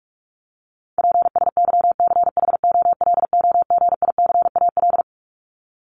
Morse code, heard as a keyed tone, about 44 wpm, 700 Hz. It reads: PSYCHOLOGICAL
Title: Top 1251-1500 Words 45wpm